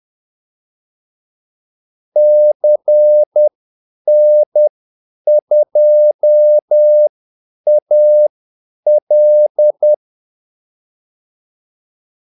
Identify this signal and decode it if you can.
Morse code, heard as a keyed tone, about 10 wpm, spaced as for 8 wpm, 600 Hz. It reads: CN2AL
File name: Callsigns-10-8wpm